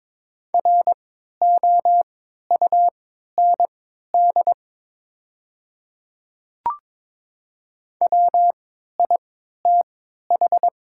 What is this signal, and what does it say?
Morse code, heard as a keyed tone, about 22 wpm, spaced as for 13 wpm, 690 Hz. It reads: ROUND WITH
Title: 3x Character Spacing - Top 200 Words - 22wpm